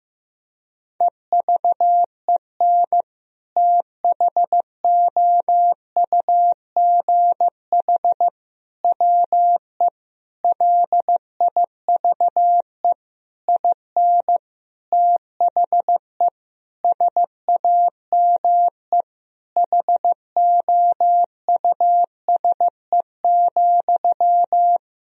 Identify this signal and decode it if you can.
Morse code, heard as a keyed tone, about 15 wpm, 700 Hz. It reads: EVEN THOUGH WE LIVE IN THE SAME HOUSE,
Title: Sentences from Top 1000 Words - Review Part 2 - 15wpm